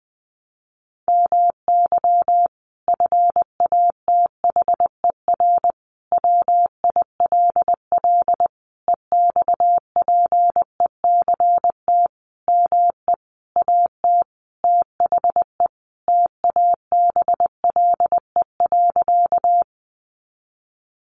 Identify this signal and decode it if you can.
Morse code, heard as a keyed tone, about 20 wpm, 700 Hz. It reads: MY FATHER WILL EXPECT ME AT THE TABLE.
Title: Sentences from Top 1100 Words 20wpm